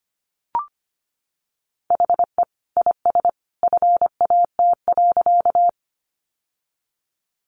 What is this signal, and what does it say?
Morse code, heard as a keyed tone, about 25 wpm, 700 Hz. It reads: HE IS FAT.
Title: Sentences from Top 600 Words 25wpm